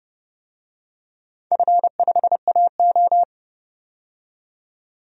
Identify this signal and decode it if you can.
Morse code, heard as a keyed tone, about 30 wpm, 700 Hz. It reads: F5AO
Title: DXCC-100 Call Signs - 4 Characters - 30wpm